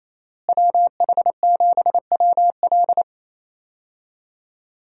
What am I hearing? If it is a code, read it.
Morse code, heard as a keyed tone, about 28 wpm, 700 Hz. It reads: WH7WL
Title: USA Call Signs - 5 Characters - Repeated 3x - 28wpm